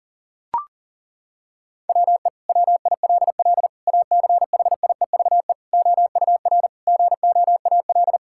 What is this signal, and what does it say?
Morse code, heard as a keyed tone, about 37 wpm, 700 Hz. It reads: WE WILL ACHIEVE OUR GOAL
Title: Sentences from Top 1000 Words - Review Part 6 - 40wpm